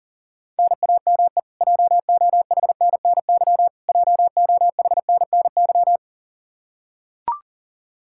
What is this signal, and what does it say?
Morse code, heard as a keyed tone, about 40 wpm, 700 Hz. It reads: NAME JOHNNY JOHNNY
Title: Practice QSO - Exchange 1 - Beginning - 40wpm